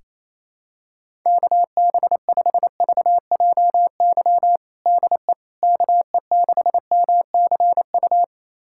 Morse code, heard as a keyed tone, about 28 wpm, 700 Hz. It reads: KB5VJY DE KE6MCU
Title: USA Callsign DE Callsign 28wpm